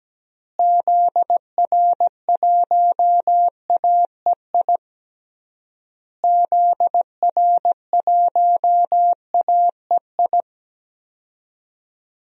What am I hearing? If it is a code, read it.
Morse code, heard as a keyed tone, about 17 wpm, 700 Hz. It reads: ZR1AEI ZR1AEI
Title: DXCC-100 Call Signs - 6 Characters - Repeated 3x - 17wpm